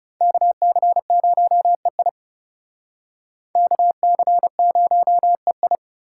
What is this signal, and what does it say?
Morse code, heard as a keyed tone, about 36 wpm, 700 Hz. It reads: KC0EI KC0EI
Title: Speed Racing - USA 2x2 Call Signs 30wpm